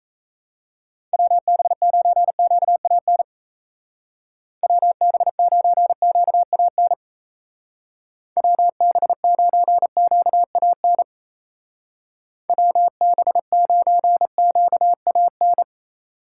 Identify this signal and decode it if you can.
Morse code, heard as a keyed tone, about 37 wpm, 700 Hz. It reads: WB9QAN WB9QAN WB9QAN WB9QAN